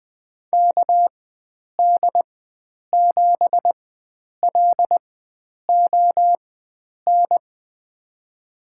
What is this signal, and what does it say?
Morse code, heard as a keyed tone, about 21 wpm, spaced as for 9 wpm, 700 Hz. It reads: KD7LON
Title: USA Call Signs - 4x Character Spacing - 20wpm